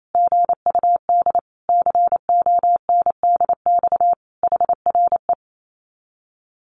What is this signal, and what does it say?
Morse code, heard as a keyed tone, about 28 wpm, 700 Hz. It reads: GUD CONDX HRE
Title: Common QSO Phrases 28wpm